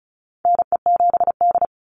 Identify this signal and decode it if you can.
Morse code, heard as a keyed tone, about 34 wpm, 700 Hz. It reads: NE7D
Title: Call Signs - 35wpm